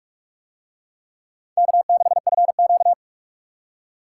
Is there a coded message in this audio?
Yes: Morse code from 1.6 to 2.9 s, heard as a keyed tone, about 44 wpm, 700 Hz. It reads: K6FQ